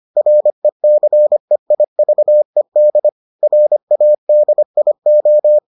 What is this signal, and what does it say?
Morse code, heard as a keyed tone, about 25 wpm, 600 Hz. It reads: RECEIVED RADIO